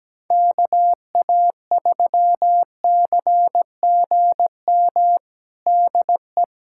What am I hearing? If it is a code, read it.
Morse code, heard as a keyed tone, about 17 wpm, 700 Hz. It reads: KA3CGM DE
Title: USA Callsign DE Callsign 17wpm